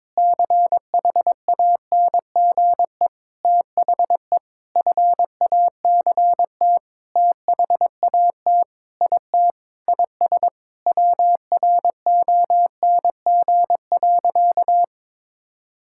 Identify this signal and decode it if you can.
Morse code, heard as a keyed tone, about 22 wpm, 700 Hz. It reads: CHANGE THE FACT THAT IT IS WRONG.
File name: Sentences with 7-Word Phrases - 22wpm